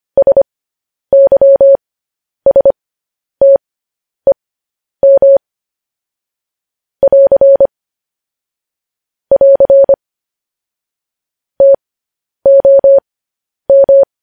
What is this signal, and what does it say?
Morse code, heard as a keyed tone, about 25 wpm, spaced as for 10 wpm, 550 Hz. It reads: SYSTEM <AR> <AR> TOM
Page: news headlines in morse code at 10 wpm